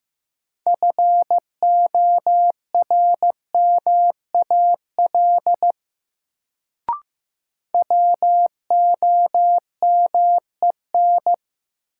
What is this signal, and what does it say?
Morse code, heard as a keyed tone, about 15 wpm, 700 Hz. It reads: FORMAL WOMEN
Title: Sentences from Top 1300 Words - New Word Drill - 15wpm